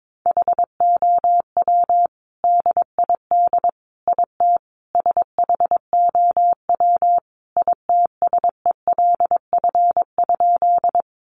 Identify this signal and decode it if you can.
Morse code, heard as a keyed tone, about 22 wpm, 700 Hz. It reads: HOW DID IT SHOW ITSELF?
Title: Sentences from Top 800 Words 22wpm